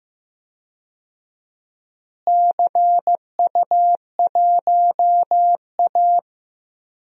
Morse code, heard as a keyed tone, about 15 wpm, 700 Hz. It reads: CU1A
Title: DXCC-100 Call Signs - 4 Characters - Repeated 3x - 15wpm